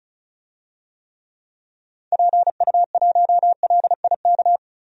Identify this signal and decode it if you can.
Morse code, heard as a keyed tone, about 36 wpm, 700 Hz. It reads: PU1LIK
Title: DXCC-100 Call Signs - 6 Characters - 35wpm